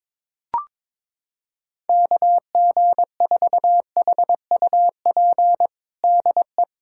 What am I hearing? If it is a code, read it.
Morse code, heard as a keyed tone, about 21 wpm, 700 Hz. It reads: KG4HUP DE
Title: USA Callsign DE Callsign - 6 Characters - 22wpm